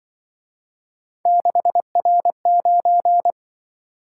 Morse code, heard as a keyed tone, about 24 wpm, 700 Hz. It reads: 6R9
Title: Speed-Racing - Three Letter-Number Combinations - 20wpm